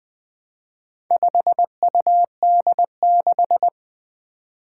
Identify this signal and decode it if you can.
Morse code, heard as a keyed tone, about 20 wpm, 700 Hz. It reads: 5UD6
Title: Four Challenging Letter-Number Combinations 20wpm